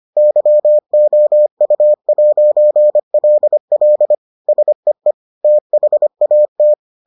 Morse code, heard as a keyed tone, about 25 wpm, 600 Hz. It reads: YOU'LL SEE THAT